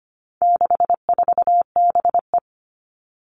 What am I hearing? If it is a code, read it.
Morse code, heard as a keyed tone, about 25 wpm, 700 Hz. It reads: 64BE
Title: All Cut Numbers - 4-Digit 25wpm